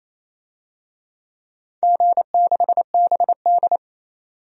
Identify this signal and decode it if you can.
Morse code, heard as a keyed tone, about 27 wpm, 700 Hz. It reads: G6BD
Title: DXCC-100 Call Signs - 4 Characters - 28wpm